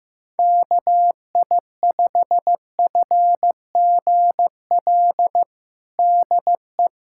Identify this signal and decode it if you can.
Morse code, heard as a keyed tone, about 15 wpm, 700 Hz. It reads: KI5FGL DE
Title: USA Callsign DE Callsign - 6 Characters - 15wpm